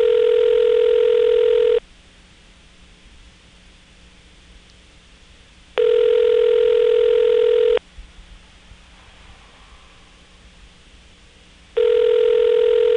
Ringback tone (North America), 440 + 480 Hz.
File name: phonetrips / Nantes1HQ